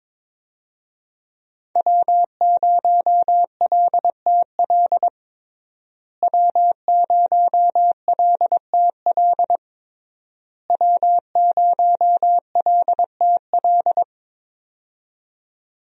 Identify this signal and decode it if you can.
Morse code, heard as a keyed tone, about 22 wpm, 700 Hz. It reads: W0LTL W0LTL W0LTL
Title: USA Call Signs - 5 Characters - Repeated 3x - 22wpm